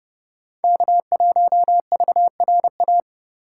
Morse code, heard as a keyed tone, about 29 wpm, 700 Hz. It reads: K1VRA